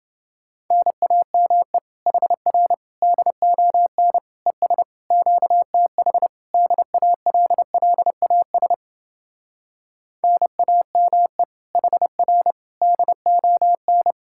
Morse code, heard as a keyed tone, about 31 wpm, 700 Hz. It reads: NAME HR DON ES QTH DALLAS NAME HR DON